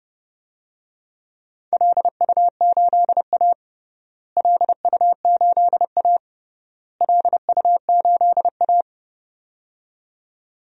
Morse code, heard as a keyed tone, about 32 wpm, 700 Hz. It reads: LU8A LU8A LU8A
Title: DXCC-100 Call Signs - 4 Characters - Repeated 3x - 30wpm